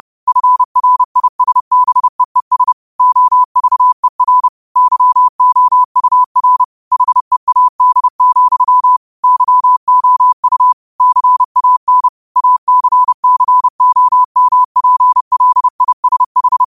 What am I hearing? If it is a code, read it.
Morse code, heard as a keyed tone, about 30 wpm, 1000 Hz. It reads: FRISBEES OVER YOUR HEAD, YOU CAN ACCOMPLISH
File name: qotdfiles / 30wpm